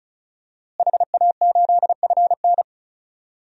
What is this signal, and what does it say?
Morse code, heard as a keyed tone, about 34 wpm, 700 Hz. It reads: HA8FN
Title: DXCC-100 Call Signs - 5 Characters - Repeated 3x - 35wpm